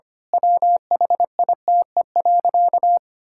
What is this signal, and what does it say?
Morse code, heard as a keyed tone, about 25 wpm, 700 Hz. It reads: WHITE.
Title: Pangram Challenge v2 - 25wpm